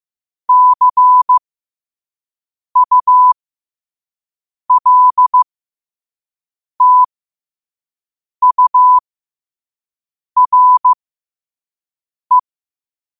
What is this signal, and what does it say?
Morse code, heard as a keyed tone, about 15 wpm, spaced as for 5 wpm, 1000 Hz. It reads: CULTURE
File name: qotdfiles / 5wpm